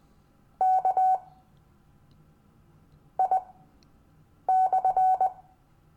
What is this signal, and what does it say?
Morse code, heard as a keyed tone, about 20 wpm, spaced as for 5 wpm, 750 Hz. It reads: KI/
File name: cw / Lesson15 five forwardslash